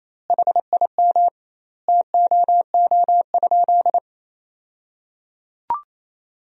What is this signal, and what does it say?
Morse code, heard as a keyed tone, about 28 wpm, 700 Hz. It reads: HIM TOO?
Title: 2x Word-Spacing - Sentences from Top 500 Words - 28wpm